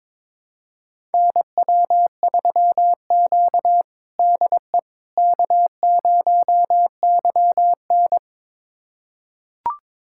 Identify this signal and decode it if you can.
Morse code, heard as a keyed tone, about 22 wpm, 700 Hz. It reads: NW3Q DE K0YN
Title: USA Callsign DE Callsign - 4 Characters - 22wpm